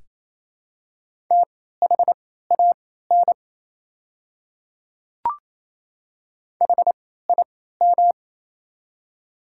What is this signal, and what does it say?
Morse code, heard as a keyed tone, about 27 wpm, spaced as for 16 wpm, 700 Hz. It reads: THAN HIM
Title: 3x Character Spacing - Top 100 Words - 28wpm